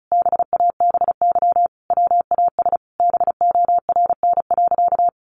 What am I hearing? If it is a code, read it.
Morse code, heard as a keyed tone, about 34 wpm, 700 Hz. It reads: BABY WAS BORN.
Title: Sentences from Top 1300 Words 35wpm